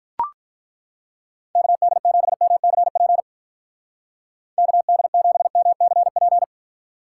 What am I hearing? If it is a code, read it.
Morse code, heard as a keyed tone, about 42 wpm, 700 Hz. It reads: KD7MCP KD7MCP
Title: Speed Racing - USA 2x3 Call Signs 35wpm